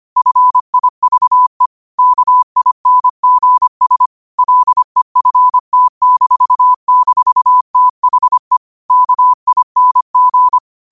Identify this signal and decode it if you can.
Morse code, heard as a keyed tone, about 25 wpm, 1000 Hz. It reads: FIVE KINGS LEFT--THE KING
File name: qotdfiles / 25wpm